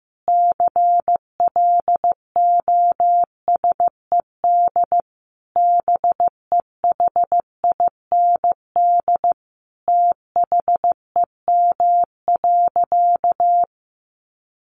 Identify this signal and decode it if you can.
Morse code, heard as a keyed tone, about 15 wpm, 700 Hz. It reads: CLOSED BEHIND THEM.
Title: Sentences from Top 1400 Words 15wpm